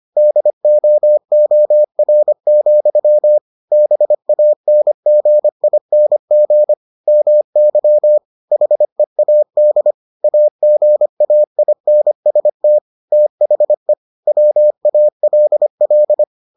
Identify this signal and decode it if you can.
Morse code, heard as a keyed tone, about 25 wpm, 600 Hz. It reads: DOOR, BANGING MY HEAD AGAINST THE WALL